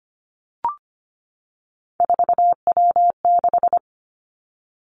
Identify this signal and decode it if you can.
Morse code, heard as a keyed tone, about 25 wpm, 700 Hz. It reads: 4W6
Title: Three Challenging Letter-Number Combinations 25wpm